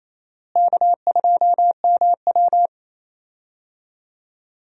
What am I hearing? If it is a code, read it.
Morse code, heard as a keyed tone, about 28 wpm, 700 Hz. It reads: K2MW
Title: USA Call Signs - 4 Characters - Repeated 2x - 28wpm